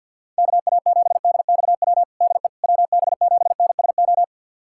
Morse code, heard as a keyed tone, about 51 wpm, 700 Hz. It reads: KA7DXW DE WB7NSO
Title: USA Callsign DE Callsign 50wpm